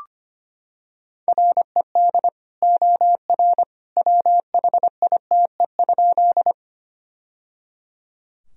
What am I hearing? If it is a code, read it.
Morse code, heard as a keyed tone, about 25 wpm, 700 Hz. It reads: RED OR WHITE?
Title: Sentences from Top 1000 Words - Review Part 2 - 25wpm